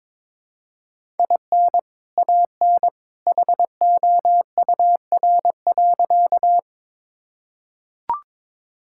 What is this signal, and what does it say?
Morse code, heard as a keyed tone, about 21 wpm, 700 Hz. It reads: IN AN HOUR.
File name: practice / 22wpm / Sentences with 2-Word Phrases - 22wpm